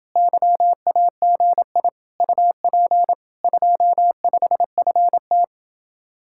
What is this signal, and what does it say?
Morse code, heard as a keyed tone, about 31 wpm, 700 Hz. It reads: YAGI UP 25FT